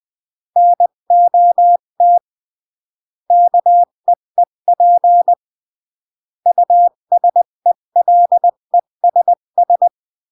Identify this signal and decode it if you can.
Morse code, heard as a keyed tone, about 20 wpm, 700 Hz. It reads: NOT KEEP USELESS
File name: CW / jodorowsky / 82 commandments 20-15